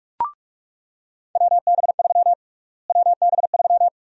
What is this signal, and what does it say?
Morse code, heard as a keyed tone, about 46 wpm, 700 Hz. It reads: WB3 WB3
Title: USA Call Sign Prefixes - Repeated 2x - 45wpm